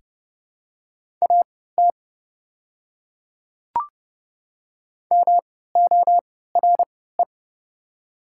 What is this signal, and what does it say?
Morse code, heard as a keyed tone, about 30 wpm, spaced as for 17 wpm, 700 Hz. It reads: AT MORE
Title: 3x Character Spacing - Top 100 Words - 30wpm